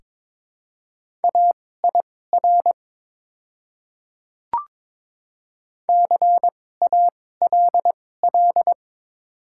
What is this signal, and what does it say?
Morse code, heard as a keyed tone, about 22 wpm, spaced as for 16 wpm, 700 Hz. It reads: AIR CALL
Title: 2x Character Spacing - Top 500 Words - 22wpm